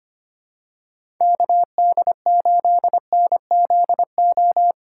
Morse code, heard as a keyed tone, about 25 wpm, 700 Hz. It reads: KD8NZO
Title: USA Call Signs - 6 Characters - 2x Character Spacing - 25wpm